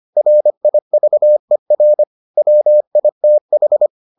Morse code, heard as a keyed tone, about 25 wpm, 600 Hz. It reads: RIVER WITH